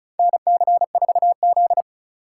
Morse code, heard as a keyed tone, about 36 wpm, 700 Hz. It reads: NC4Z
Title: USA Call Signs - 2x Character Spacing - 35wpm